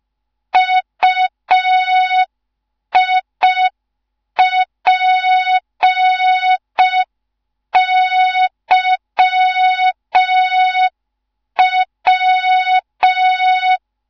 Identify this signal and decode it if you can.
Morse code, heard as a keyed tone, about 5 wpm, 3700 Hz. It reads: UIPYW
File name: Morze-HA2MN / hang / 15 GYAKORLAT